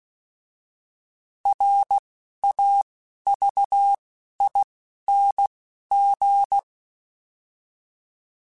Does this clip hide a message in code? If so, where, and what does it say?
Yes: Morse code from 1.5 to 6.6 s, heard as a keyed tone, about 16 wpm, spaced as for 11 wpm, 800 Hz. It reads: RAVING